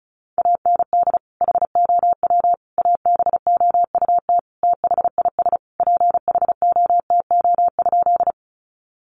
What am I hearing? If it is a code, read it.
Morse code, heard as a keyed tone, about 34 wpm, 700 Hz. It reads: AND HOW ABOUT THIS PHOTO?
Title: Sentences from Top 1200 Words 35wpm